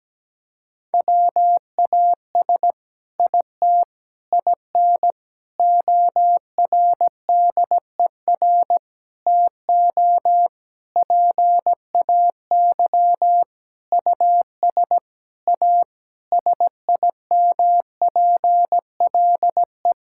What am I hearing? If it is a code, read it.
Morse code, heard as a keyed tone, about 17 wpm, 700 Hz. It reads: WAS IT IN ORDER TO PAY US A SIMPLE
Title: Sentences from Top 1000 Words - Review Part 6 - 17wpm